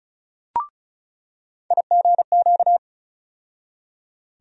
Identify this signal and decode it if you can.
Morse code, heard as a keyed tone, about 36 wpm, 700 Hz. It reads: IGQ
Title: USA Call Sign Postfixes - 3 Letters 35wpm